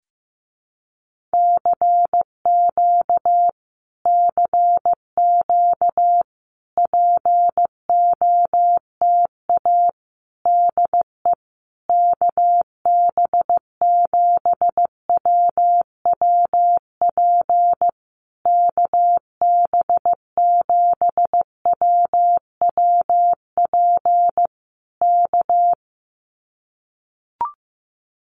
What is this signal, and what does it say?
Morse code, heard as a keyed tone, about 14 wpm, 700 Hz. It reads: CQ CQ POTA DE KB7WWP KB7WWP K
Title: Common QSO Phrases Encore 15wpm